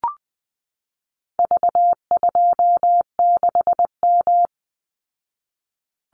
Morse code, heard as a keyed tone, about 20 wpm, 700 Hz. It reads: V26M
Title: Call Signs Repeated 2x - 20wpm